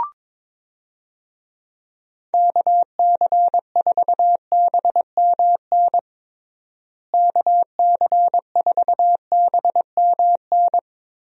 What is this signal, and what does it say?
Morse code, heard as a keyed tone, about 22 wpm, 700 Hz. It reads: KC4BMN KC4BMN
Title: USA Call Signs - 6 Characters - Repeated 3x - 22wpm